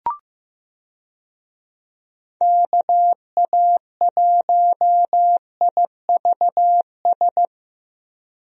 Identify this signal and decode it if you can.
Morse code, heard as a keyed tone, about 15 wpm, 700 Hz. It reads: KA1IVS
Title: USA Call Signs - 6 Characters - Repeated 2x - 15wpm